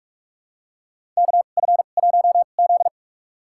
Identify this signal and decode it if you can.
Morse code, heard as a keyed tone, about 44 wpm, spaced as for 32 wpm, 700 Hz. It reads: KF1Z
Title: USA Call Signs - 4 Characters - 2x Character Spacing - 45wpm